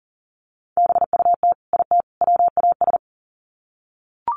Morse code, heard as a keyed tone, about 40 wpm, 700 Hz. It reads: BUT IT WAS
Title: Sets of 3 Words 40wpm